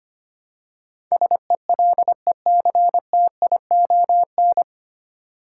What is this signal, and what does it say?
Morse code, heard as a keyed tone, about 25 wpm, 700 Hz. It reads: SELECTION
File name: practice / 25wpm / 9 Letter Words - First 3 Letters Given - 25wpm